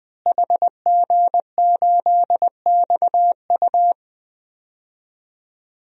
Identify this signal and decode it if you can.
Morse code, heard as a keyed tone, about 20 wpm, 700 Hz. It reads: HG8XU
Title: DXCC-100 Call Signs - 5 Characters - Repeated 3x - 20wpm